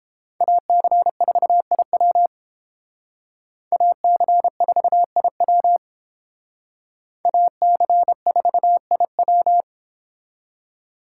Morse code, heard as a keyed tone, about 31 wpm, 700 Hz. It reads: AC4IW AC4IW AC4IW